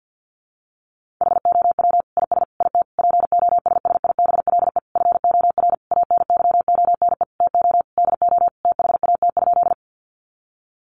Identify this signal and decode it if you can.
Morse code, heard as a keyed tone, about 50 wpm, 700 Hz. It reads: HOW IS IT POSSIBLE FOR ANYONE TO DO THAT?
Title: Sentences from Top 900 Words 50wpm